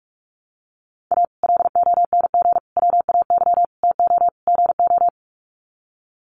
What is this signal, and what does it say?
Morse code, heard as a keyed tone, about 45 wpm, 700 Hz. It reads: A LONG WAY TO GO
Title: Sets of 5 Words 45wpm